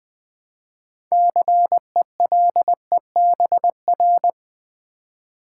Morse code, heard as a keyed tone, about 20 wpm, 700 Hz. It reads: CELEBR
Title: Build A Word - 3 Letters At A Time - 9-Letter Words - 20wpm